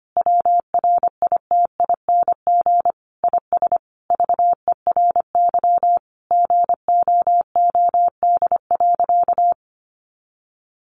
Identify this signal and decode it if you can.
Morse code, heard as a keyed tone, about 25 wpm, 700 Hz. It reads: WRITING IS VERY GOOD.
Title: Sentences from Top 800 Words 25wpm